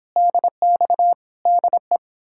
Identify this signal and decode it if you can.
Morse code, heard as a keyed tone, about 32 wpm, 700 Hz. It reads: DX DE